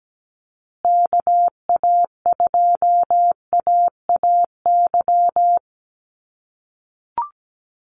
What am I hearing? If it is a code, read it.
Morse code, heard as a keyed tone, about 17 wpm, 700 Hz. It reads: KA2AAY
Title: USA Call Signs Repeated 3x 17wpm